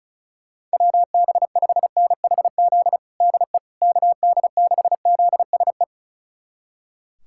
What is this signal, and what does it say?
Morse code, heard as a keyed tone, about 36 wpm, 700 Hz. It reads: WB5NHZ DE KD6ZSE
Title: USA Callsign DE Callsign 35wpm